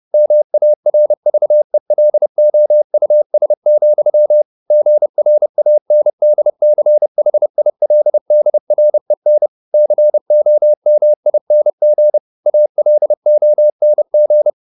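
Morse code, heard as a keyed tone, about 30 wpm, 600 Hz. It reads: MARVELOUS, GRANDCHILDREN COMING ALONG
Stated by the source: Morse code at 30 wpm, the novel Never by Ken Follett in English, from CW med Gnister.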